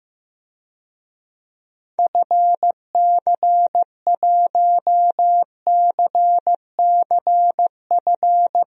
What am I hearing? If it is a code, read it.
Morse code, heard as a keyed tone, about 16 wpm, 700 Hz. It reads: FC1CCF
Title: DXCC-100 Call Signs - 6 Characters - 15wpm